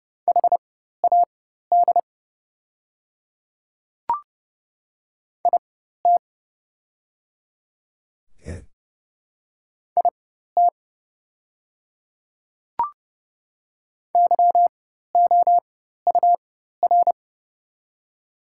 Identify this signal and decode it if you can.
Morse code, heard as a keyed tone, about 30 wpm, spaced as for 14 wpm, 690 Hz. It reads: HAD IT IT YOUR
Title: 4x Character Spacing - Top 100 Words - 30wpm